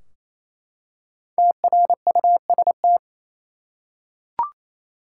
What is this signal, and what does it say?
Morse code, heard as a keyed tone, about 28 wpm, 700 Hz. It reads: TRUST
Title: Five Letter Words - 28wpm